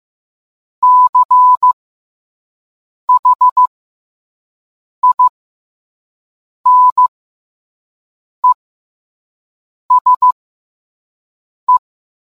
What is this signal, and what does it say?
Morse code, heard as a keyed tone, about 15 wpm, spaced as for 5 wpm, 1000 Hz. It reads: CHINESE